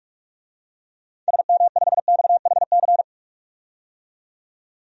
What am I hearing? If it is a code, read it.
Morse code, heard as a keyed tone, about 46 wpm, 700 Hz. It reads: SM5XHC
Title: DXCC-100 Call Signs - 6 Characters - 45wpm